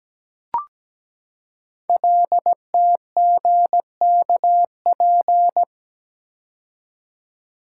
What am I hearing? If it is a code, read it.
Morse code, heard as a keyed tone, about 17 wpm, 700 Hz. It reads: LTGKP
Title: Code Groups - 5 Letter:Numbers - 17wpm